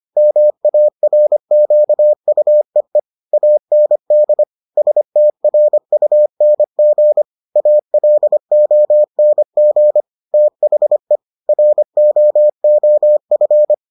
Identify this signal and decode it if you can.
Morse code, heard as a keyed tone, about 25 wpm, 600 Hz. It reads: MARQUEE AND STRUNG ALONG THE ROOF